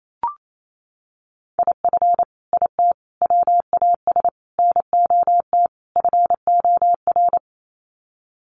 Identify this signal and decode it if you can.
Morse code, heard as a keyed tone, about 28 wpm, 700 Hz. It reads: IF IT WAS NOT FOR
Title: Sets of 5 Words Encore 28wpm